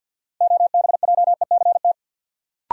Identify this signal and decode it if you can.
Morse code, heard as a keyed tone, about 53 wpm, 700 Hz. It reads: OBJECT
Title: Sentences from Top 500 Words - New Word Drill - 50wpm